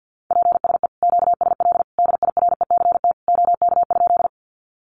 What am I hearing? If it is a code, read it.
Morse code, heard as a keyed tone, about 50 wpm, 700 Hz. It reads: PSE QSL DIRECT OK?
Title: Common QSO Phrases 50wpm